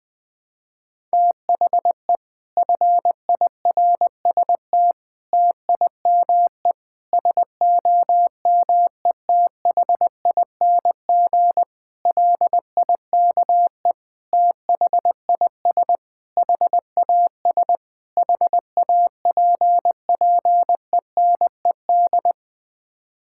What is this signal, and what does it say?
Morse code, heard as a keyed tone, about 20 wpm, 700 Hz. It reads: THE FIRST TIME SOMETHING LIKE THIS HAS HAPPENED
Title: Sets of 8 Words - 20wpm